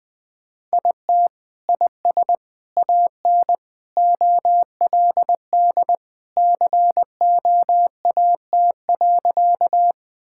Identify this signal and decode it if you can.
Morse code, heard as a keyed tone, about 20 wpm, 700 Hz. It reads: IT IS AN OLD COAT.
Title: Sentences from Top 1600 Words 20wpm